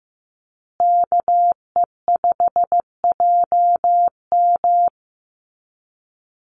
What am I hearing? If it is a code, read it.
Morse code, heard as a keyed tone, about 15 wpm, 700 Hz. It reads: KE5JM
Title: USA Call Signs Repeated 2x 15wpm